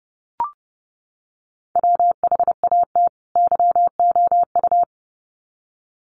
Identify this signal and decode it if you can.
Morse code, heard as a keyed tone, about 30 wpm, 700 Hz. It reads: WHAT YOU
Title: Sets of 2 Words 30wpm